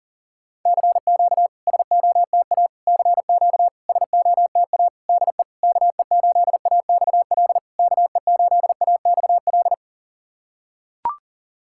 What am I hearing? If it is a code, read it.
Morse code, heard as a keyed tone, about 38 wpm, 690 Hz. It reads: CQ SOTA CQ SOTA DE KE8AXL KE8AXL
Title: SOTA QSO Practice 40wpm